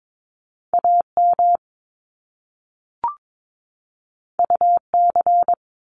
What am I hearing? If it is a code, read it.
Morse code, heard as a keyed tone, about 22 wpm, 700 Hz. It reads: AM UC
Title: Top Two Letter Combinations in Words 22wpm